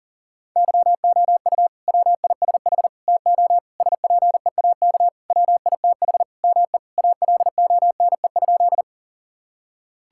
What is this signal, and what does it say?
Morse code, heard as a keyed tone, about 40 wpm, 700 Hz. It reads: YOU WISH TO SPEAK WITH ME ALONE?
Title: Sentences from Top 1600 Words 40wpm